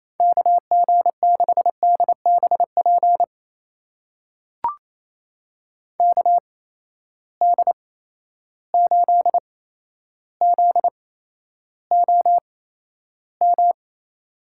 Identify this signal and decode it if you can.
Morse code, heard as a keyed tone, about 28 wpm, spaced as for 12 wpm, 700 Hz. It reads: KG6DBP KD8ZOM
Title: USA Call Signs - 6 Characters - 8x Character Spacing - 28wpm